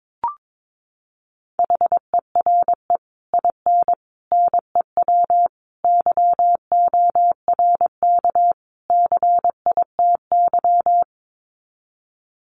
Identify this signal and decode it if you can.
Morse code, heard as a keyed tone, about 22 wpm, 700 Hz. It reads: HERE IN NEW YORK CITY
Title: Sets of 5 Words Encore2 22wpm